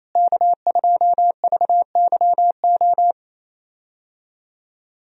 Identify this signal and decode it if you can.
Morse code, heard as a keyed tone, about 27 wpm, 700 Hz. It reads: K2VYO
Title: USA Call Signs - 5 Characters - Repeated 3x - 28wpm